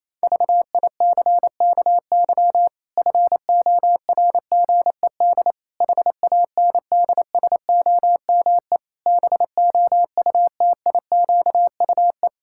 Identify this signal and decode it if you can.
Morse code, heard as a keyed tone, about 27 wpm, 700 Hz. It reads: VICKY FORGED HANDSOME BOUTIQUE
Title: Pangram Challenge v2 - 28wpm